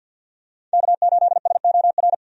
Morse code, heard as a keyed tone, about 46 wpm, 700 Hz. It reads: K8SOR
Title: USA Call Signs - 5 Characters - 50wpm